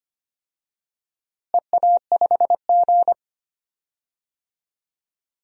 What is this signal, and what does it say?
Morse code, heard as a keyed tone, about 25 wpm, 700 Hz. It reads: EA5G
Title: DXCC-100 Call Signs - 4 Characters - Repeated 2x - 25wpm